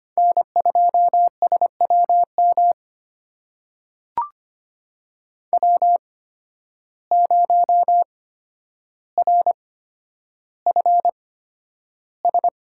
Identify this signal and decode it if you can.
Morse code, heard as a keyed tone, about 25 wpm, spaced as for 11 wpm, 700 Hz. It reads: N2SWM W0RFS
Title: USA Call Signs - 5 Characters - 8x Character Spacing - 25wpm